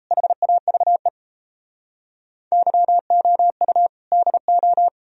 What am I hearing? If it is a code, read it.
Morse code, heard as a keyed tone, about 36 wpm, 700 Hz. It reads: HAVE YOU DO